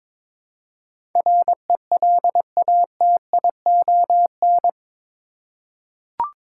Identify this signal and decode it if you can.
Morse code, heard as a keyed tone, about 22 wpm, 700 Hz. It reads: RELATION
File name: practice / 22wpm / Eight Letter Words - 22wpm